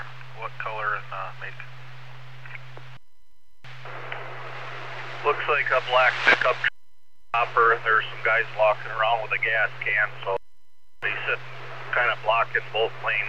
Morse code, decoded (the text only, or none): none